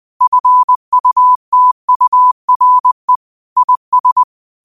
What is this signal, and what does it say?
Morse code, heard as a keyed tone, about 20 wpm, 1000 Hz. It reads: FUTURE IS
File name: qotdfiles / 20wpm